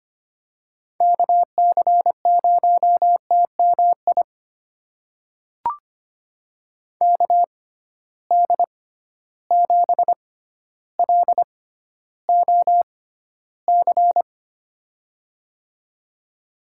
Morse code, heard as a keyed tone, about 25 wpm, spaced as for 13 wpm, 700 Hz. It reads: KC0TMI KD7LOC
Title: USA Call Signs - 6 Characters - 6x Character Spacing - 25wpm